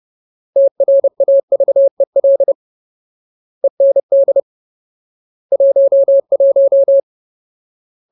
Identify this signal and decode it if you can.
Morse code, heard as a keyed tone, about 30 wpm, 550 Hz. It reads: TRAVEL END 11
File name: Ultimate Word List / 30WPM Ultimate